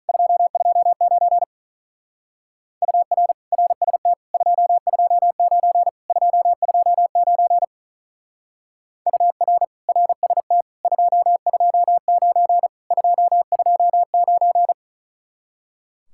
Morse code, heard as a keyed tone, about 43 wpm, 700 Hz. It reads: 229 UR RST 229 229 UR RST 229 229
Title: Speed-Racing Common QSO Phrases - 35wpm